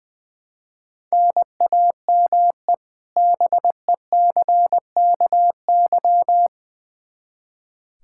Morse code, heard as a keyed tone, about 20 wpm, 700 Hz. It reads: NAME BECKY
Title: Practice QSO - Exchange 1 - Beginning - 20wpm